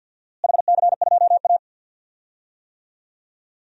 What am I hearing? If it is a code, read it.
Morse code, heard as a keyed tone, about 52 wpm, 700 Hz. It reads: HC2A
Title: DXCC-100 Call Signs - 4 Characters - Repeated 3x - 50wpm